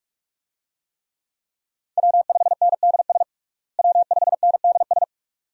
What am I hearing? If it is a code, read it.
Morse code, heard as a keyed tone, about 42 wpm, 700 Hz. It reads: W5NDS W5NDS
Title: USA Call Signs - 5 Characters - Repeated 2x - 45wpm